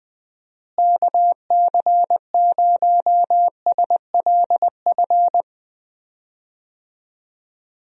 Morse code, heard as a keyed tone, about 20 wpm, 700 Hz. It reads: KC0SLF
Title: USA Call Signs - 6 Characters - Repeated 3x - 20wpm